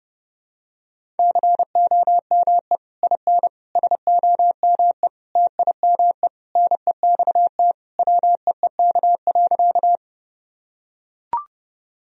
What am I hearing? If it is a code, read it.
Morse code, heard as a keyed tone, about 30 wpm, 700 Hz. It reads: COME IN SOME TIME NEXT WEEK.
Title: Sentences from Top 1000 Words - Review Part 2 - 30wpm